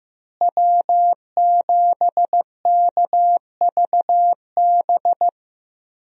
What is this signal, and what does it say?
Morse code, heard as a keyed tone, about 15 wpm, 700 Hz. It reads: W7KVB
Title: USA Call Signs - 5 Characters - Repeated 3x - 15wpm